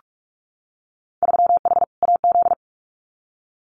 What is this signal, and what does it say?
Morse code, heard as a keyed tone, about 44 wpm, 700 Hz. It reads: 3H AZ
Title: Winter Field Day Exchanges Encore 45wpm